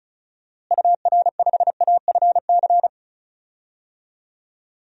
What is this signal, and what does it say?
Morse code, heard as a keyed tone, about 36 wpm, 700 Hz. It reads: UR5AFC
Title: DXCC-100 Call Signs - 6 Characters - Repeated 2x - 35wpm